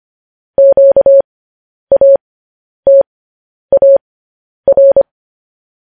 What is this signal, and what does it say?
Morse code, heard as a keyed tone, about 25 wpm, spaced as for 10 wpm, 550 Hz. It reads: QATAR
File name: podcast / News headlines-1x-25x10